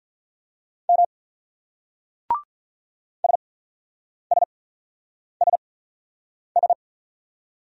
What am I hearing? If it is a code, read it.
Morse code, heard as a keyed tone, about 45 wpm, spaced as for 9 wpm, 700 Hz. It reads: M SSSS